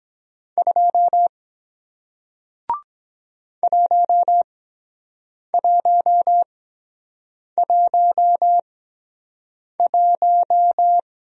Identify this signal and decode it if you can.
Morse code, heard as a keyed tone, about 23 wpm, spaced as for 7 wpm, 700 Hz. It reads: 2 1111